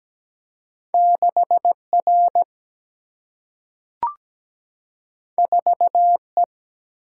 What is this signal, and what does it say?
Morse code, heard as a keyed tone, about 17 wpm, 700 Hz. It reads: 6R 4E
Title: Single-Letter Single-Number 17wpm